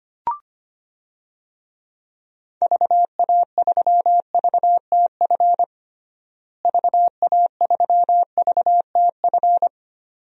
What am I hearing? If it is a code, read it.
Morse code, heard as a keyed tone, about 25 wpm, 700 Hz. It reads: VA3VTF VA3VTF
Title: DXCC-100 Call Signs - 6 Characters - Repeated 2x - 25wpm